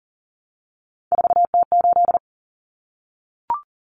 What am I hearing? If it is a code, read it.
Morse code, heard as a keyed tone, about 40 wpm, 700 Hz. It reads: 4T8
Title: Common Cut Numbers - 3-Digit 40wpm